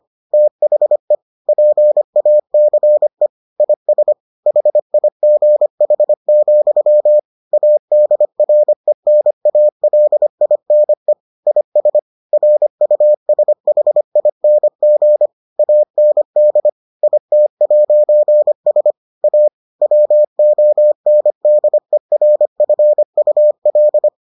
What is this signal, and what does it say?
Morse code, heard as a keyed tone, about 25 wpm, 600 Hz. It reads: THE PACE IS HIGH, ADRENALINE IS RUSHING AND IT'S A WONDERFUL